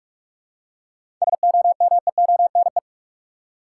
Morse code, heard as a keyed tone, about 47 wpm, 700 Hz. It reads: SOMEONE
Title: Seven Letter Words - 45wpm